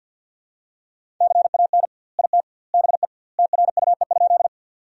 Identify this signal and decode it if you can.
Morse code, heard as a keyed tone, about 51 wpm, 700 Hz. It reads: CAN IT BE TRUE?
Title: 2x Word-Spacing - Sentences from Top 400 Words - 50wpm